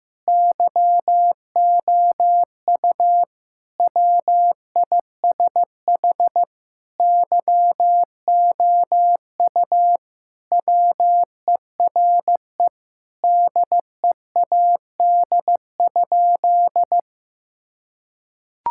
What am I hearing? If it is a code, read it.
Morse code, heard as a keyed tone, about 15 wpm, 700 Hz. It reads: YOU WISH YOU WERE DEAD?
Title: Sentences from Top 1600 Words 15wpm